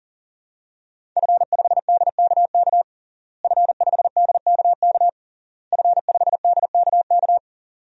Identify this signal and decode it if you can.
Morse code, heard as a keyed tone, about 37 wpm, 700 Hz. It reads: F5DKK F5DKK F5DKK
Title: DXCC-100 Call Signs - 5 Characters - Repeated 3x - 40wpm